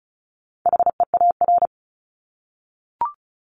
Morse code, heard as a keyed tone, about 34 wpm, 700 Hz. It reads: HEAR
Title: Top 500 Words Repeated 35wpm